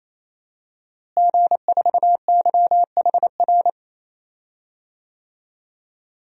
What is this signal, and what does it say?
Morse code, heard as a keyed tone, about 28 wpm, 700 Hz. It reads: G4YHR
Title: DXCC-100 Call Signs - 5 Characters - Repeated 3x - 28wpm